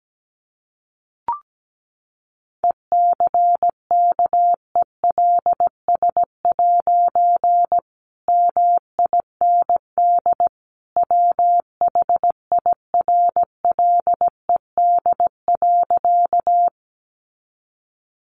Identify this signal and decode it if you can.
Morse code, heard as a keyed tone, about 17 wpm, 700 Hz. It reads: ECKELS' MIND WHIRLED.
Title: A Sound Of Thunder 17wpm